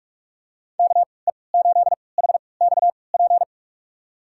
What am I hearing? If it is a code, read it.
Morse code, heard as a keyed tone, about 46 wpm, spaced as for 25 wpm, 700 Hz. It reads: KE8HXP
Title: USA Call Signs - 6 Characters - 3x Character Spacing - 45wpm